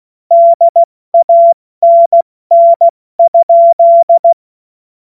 Morse code, heard as a keyed tone, about 16 wpm, 680 Hz. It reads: DANN?
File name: Morsekurs / Witze / Witz 12 März 2024 70 F80